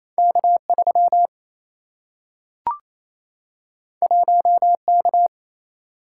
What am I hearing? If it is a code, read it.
Morse code, heard as a keyed tone, about 28 wpm, 700 Hz. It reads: K3 1K
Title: Single-Letter Single-Number 28wpm